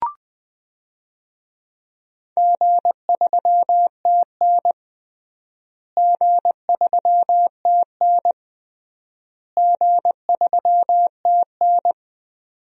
Morse code, heard as a keyed tone, about 20 wpm, 700 Hz. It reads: G3TN G3TN G3TN
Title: DXCC-100 Call Signs - 4 Characters - Repeated 3x - 20wpm